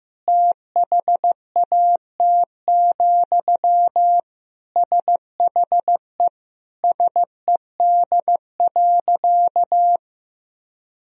Morse code, heard as a keyed tone, about 15 wpm, 700 Hz. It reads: THAT, SHE SED.
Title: Sentences from Top 200 Words and Abbreviations 15wpm